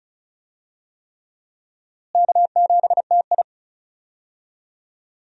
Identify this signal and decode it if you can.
Morse code, heard as a keyed tone, about 34 wpm, 700 Hz. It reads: K7TI
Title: USA Call Signs - 4 Characters - Repeated 3x - 35wpm